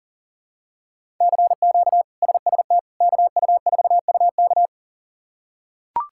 Morse code, heard as a keyed tone, about 39 wpm, 700 Hz. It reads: CQ SST KU4UK
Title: K1USN SST QSO Practice - 40wpm